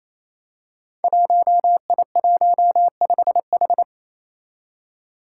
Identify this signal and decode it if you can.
Morse code, heard as a keyed tone, about 29 wpm, 700 Hz. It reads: 1I15H
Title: Five Challenging Letter-Number Combinations 28wpm